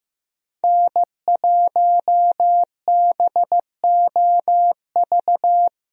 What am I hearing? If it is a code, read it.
Morse code, heard as a keyed tone, about 15 wpm, 700 Hz. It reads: N1BOV
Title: USA Call Signs - 5 Characters - Repeated 2x - 15wpm